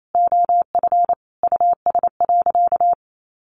Morse code, heard as a keyed tone, about 28 wpm, 700 Hz. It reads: OF US.
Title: Sentences from Top 1100 Words 28wpm